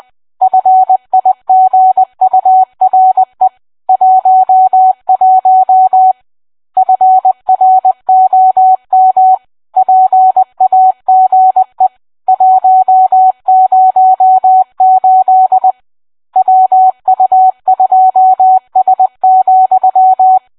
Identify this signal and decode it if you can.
Morse code, heard as a keyed tone, about 20 wpm, 750 Hz. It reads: FIGURE 11 FROM PAGE 108 WU2S,